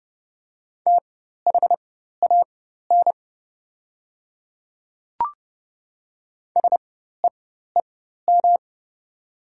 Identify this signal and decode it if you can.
Morse code, heard as a keyed tone, about 30 wpm, spaced as for 14 wpm, 690 Hz. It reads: THAN SEEM
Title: 4x Character Spacing - Top 500 Words - 30wpm